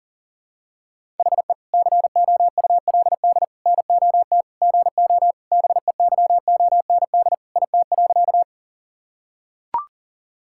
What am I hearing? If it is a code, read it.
Morse code, heard as a keyed tone, about 37 wpm, 700 Hz. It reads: HE COULD NOT GO BEYOND IT.
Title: Sentences from Top 1000 Words - Review Part 6 - 40wpm